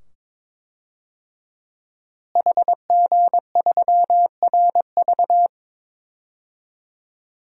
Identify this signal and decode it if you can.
Morse code, heard as a keyed tone, about 22 wpm, 700 Hz. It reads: HG3RV
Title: DXCC-100 Call Signs - 5 Characters - Repeated 2x - 22wpm